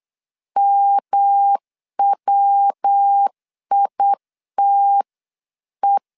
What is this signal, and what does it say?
Morse code, heard as a keyed tone, about 8 wpm, 800 Hz. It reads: MWITE